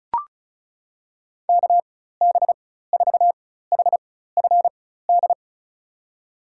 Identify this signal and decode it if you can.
Morse code, heard as a keyed tone, about 36 wpm, spaced as for 16 wpm, 700 Hz. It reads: KB4HFD
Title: USA Call Signs - 4x Character Spacing - 35wpm